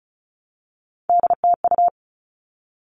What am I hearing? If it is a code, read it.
Morse code, heard as a keyed tone, about 34 wpm, 700 Hz. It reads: DTU
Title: All Cut Numbers - 3-Digit 35wpm